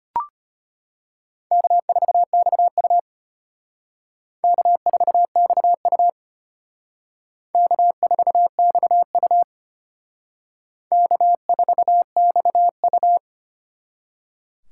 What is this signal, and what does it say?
Morse code, heard as a keyed tone, about 33 wpm, 700 Hz. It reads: K4XU K4XU K4XU K4XU